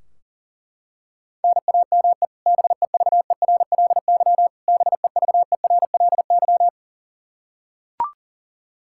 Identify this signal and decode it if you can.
Morse code, heard as a keyed tone, about 37 wpm, 700 Hz. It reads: NAME BEVERLY BEVERLY
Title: Practice QSO - Exchange 1 - Beginning - 40wpm